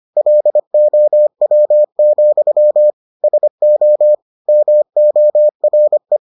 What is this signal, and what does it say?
Morse code, heard as a keyed tone, about 25 wpm, 600 Hz. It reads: LOW, SO MORE